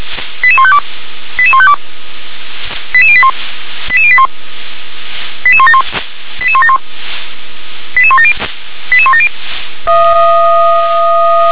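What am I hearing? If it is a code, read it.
Five-tone selective call (ZVEI), heard as quick runs of five tones, tones of about 70 ms. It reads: 88152 88152 88081 88081 88171 88171 88170 88170
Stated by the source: ZVEI five-tone call received on SDR.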